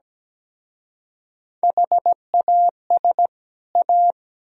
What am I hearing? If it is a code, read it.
Morse code, heard as a keyed tone, about 20 wpm, 700 Hz. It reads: HAS A
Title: Speed-Racing - Sets of 2 Words 17wpm